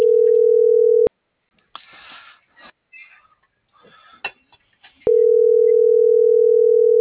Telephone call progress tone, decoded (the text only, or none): ringback tone